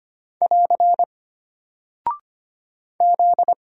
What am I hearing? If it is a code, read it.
Morse code, heard as a keyed tone, about 25 wpm, 700 Hz. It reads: <AR> Z
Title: Single Character (Letter:Number:Punctuation:Prosign) Alt 25wpm